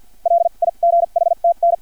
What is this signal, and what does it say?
Morse code, heard as a keyed tone, about 50 wpm, spaced as for 39 wpm, 680 Hz. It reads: LIGHTN
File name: archived / CRSnet